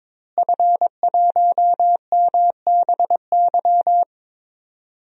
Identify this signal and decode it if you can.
Morse code, heard as a keyed tone, about 22 wpm, 700 Hz. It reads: F1MBY
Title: DXCC-100 Call Signs - 5 Characters - Repeated 3x - 22wpm